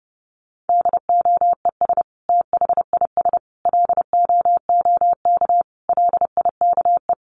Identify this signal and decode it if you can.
Morse code, heard as a keyed tone, about 30 wpm, 700 Hz. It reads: DOES THIS LOOK LIKE
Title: Sentences from Top 1500 Words 30wpm